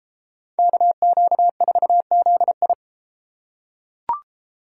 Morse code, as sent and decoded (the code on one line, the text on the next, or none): -.- --.- ....- --.. ..
KQ4ZI